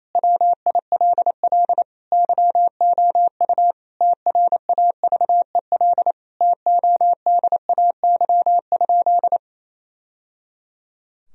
Morse code, heard as a keyed tone, about 27 wpm, 700 Hz. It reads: WILL YOU TRAVEL TODAY?
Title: Sentences from Top 1000 Words - Review Part 2 - 28wpm